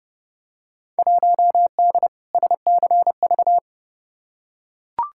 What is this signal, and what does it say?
Morse code, heard as a keyed tone, about 32 wpm, 700 Hz. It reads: 1D SCV
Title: Field Day Exchanges Encore - 30wpm